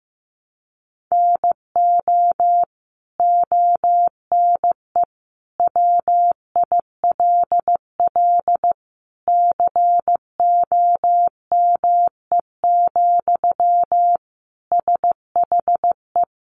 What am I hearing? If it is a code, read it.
Morse code, heard as a keyed tone, about 15 wpm, 700 Hz. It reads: NO ONE WILL COME, SHE
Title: Sentences from Top 100 Words 15wpm